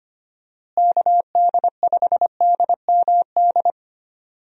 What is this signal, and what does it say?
Morse code, heard as a keyed tone, about 25 wpm, 700 Hz. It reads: KD5DMD
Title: USA Call Signs - 6 Characters - Repeated 2x - 25wpm